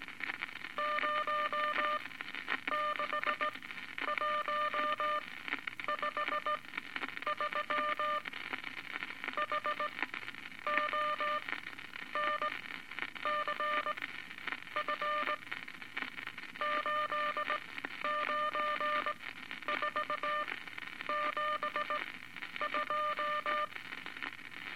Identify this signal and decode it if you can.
Morse code, heard as a keyed tone, about 18 wpm, spaced as for 9 wpm, 1200 Hz, with noise. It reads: 06153 HONCF 89472
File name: Eight Words Per Minute